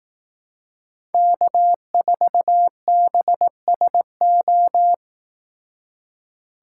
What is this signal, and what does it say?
Morse code, heard as a keyed tone, about 18 wpm, 700 Hz. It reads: K4BSO
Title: Speed Racing - USA 1x3 Call Signs 15wpm